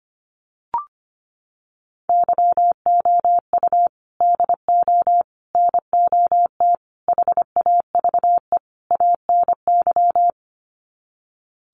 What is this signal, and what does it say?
Morse code, heard as a keyed tone, about 25 wpm, 700 Hz. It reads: YOU DO NOT HAVE ANY
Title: Sets of 5 Words Encore 25wpm